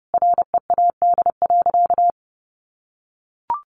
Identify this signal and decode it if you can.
Morse code, heard as a keyed tone, about 30 wpm, 700 Hz. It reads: READ.
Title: Sentences from Top 1300 Words 30wpm